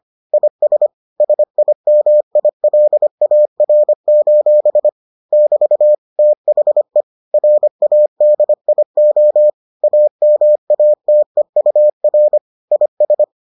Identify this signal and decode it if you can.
Morse code, heard as a keyed tone, about 25 wpm, 600 Hz. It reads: IS SIMILAR: <BT> THE RADIO AMATEUR IS